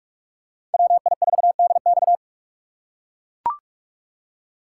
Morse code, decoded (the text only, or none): WI4DX